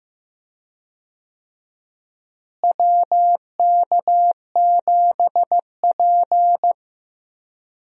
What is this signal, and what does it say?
Morse code, heard as a keyed tone, about 15 wpm, 700 Hz. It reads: WK7P
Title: USA Call Signs - 4 Characters - Repeated 2x - 15wpm